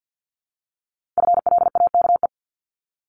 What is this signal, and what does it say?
Morse code, heard as a keyed tone, about 50 wpm, 700 Hz. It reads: FLAKE
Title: Multi-letter Phonemes Repeated 3x 50wpm